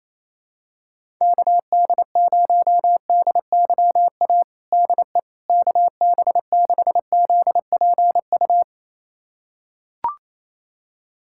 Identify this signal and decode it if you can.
Morse code, heard as a keyed tone, about 28 wpm, 700 Hz. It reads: KD0DYA DE KB6ZPU
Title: USA Callsign DE Callsign 28wpm